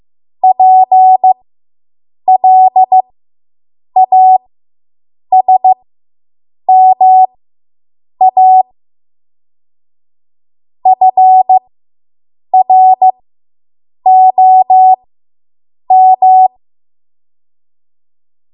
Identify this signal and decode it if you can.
Morse code, heard as a keyed tone, about 15 wpm, spaced as for 7 wpm, 750 Hz. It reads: PLASMA FROM